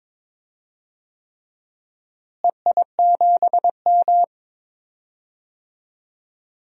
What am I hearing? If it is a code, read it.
Morse code, heard as a keyed tone, about 22 wpm, 700 Hz. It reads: EI7M
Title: DXCC-100 Call Signs - 4 Characters - Repeated 3x - 22wpm